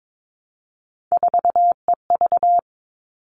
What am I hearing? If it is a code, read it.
Morse code, heard as a keyed tone, about 22 wpm, 700 Hz. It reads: 4EV
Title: All Cut Numbers - 3-Digit 22wpm